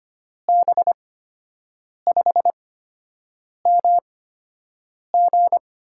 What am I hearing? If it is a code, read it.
Morse code, heard as a keyed tone, about 25 wpm, spaced as for 7 wpm, 700 Hz. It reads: B5MG
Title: USA Call Signs - 6 Characters - 8x Character Spacing - 25wpm